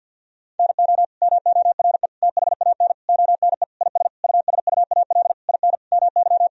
Morse code, heard as a keyed tone, about 52 wpm, 700 Hz. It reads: NO MORE THAN ONE IS USUAL IN MY
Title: Sentences from Top 1000 Words - Review Part 2 - 50wpm